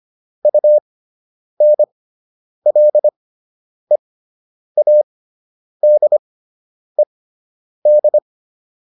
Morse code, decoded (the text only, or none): UNLEADED